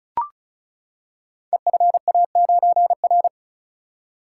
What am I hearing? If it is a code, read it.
Morse code, heard as a keyed tone, about 36 wpm, 700 Hz. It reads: EFA9R
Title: Code Groups - 5 Letter:Numbers - 35wpm